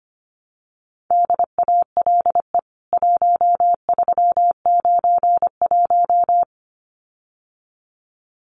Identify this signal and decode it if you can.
Morse code, heard as a keyed tone, about 25 wpm, 700 Hz. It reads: DALE 1391
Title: CWT Exchange 25wpm